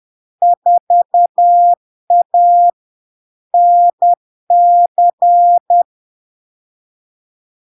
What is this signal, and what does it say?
Morse code, heard as a keyed tone, about 10 wpm, 700 Hz. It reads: 4A NC
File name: LARK-FD-2022-Log-in-CW-10wpm-700-Hz